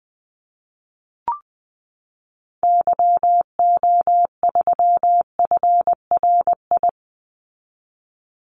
Morse code, heard as a keyed tone, about 20 wpm, 700 Hz. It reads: YO3FRI
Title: Call Signs Repeated 2x - 20wpm